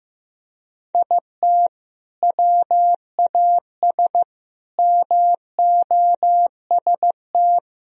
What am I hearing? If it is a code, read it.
Morse code, heard as a keyed tone, about 16 wpm, 700 Hz. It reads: IT WAS MOST
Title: Sentences from Top 1000 Words - Review Part 1 - 15wpm